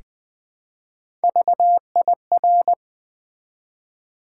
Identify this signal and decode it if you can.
Morse code, heard as a keyed tone, about 20 wpm, 700 Hz. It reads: VIR
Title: Build A Word - 1 Letter At A Time - 5-Letter Words - 20wpm